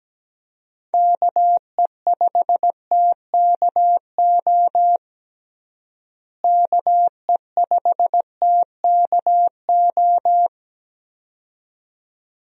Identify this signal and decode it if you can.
Morse code, heard as a keyed tone, about 17 wpm, 700 Hz. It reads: KE5TKO KE5TKO
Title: USA Call Signs - 6 Characters - Repeated 2x - 17wpm